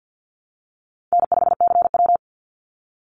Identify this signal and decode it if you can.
Morse code, heard as a keyed tone, about 50 wpm, 700 Hz. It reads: N5CW
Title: Call Signs Repeated 3x - 50wpm